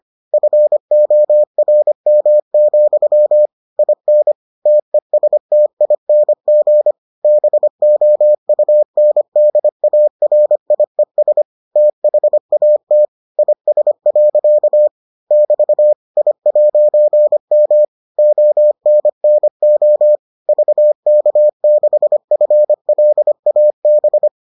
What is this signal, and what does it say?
Morse code, heard as a keyed tone, about 25 wpm, 600 Hz. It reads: FORM, IN TESTING BOUNDARIES THAT IS. <BT> I'M ONNO VK6FLAB